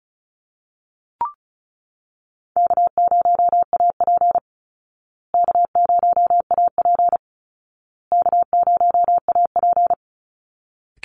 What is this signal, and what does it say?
Morse code, heard as a keyed tone, about 34 wpm, 700 Hz. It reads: K0AP K0AP K0AP
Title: Easy Call Signs Repeated 3x - 35wpm